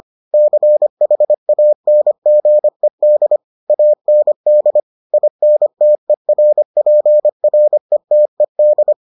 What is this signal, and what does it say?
Morse code, heard as a keyed tone, about 25 wpm, 600 Hz. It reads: CHANGED AND INTERPRETED